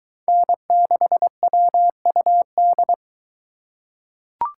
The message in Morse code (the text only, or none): N6WUD